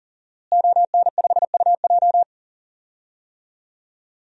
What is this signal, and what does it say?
Morse code, heard as a keyed tone, about 40 wpm, 700 Hz. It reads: ON5UJ
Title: DXCC-100 Call Signs - 5 Characters - Repeated 2x - 40wpm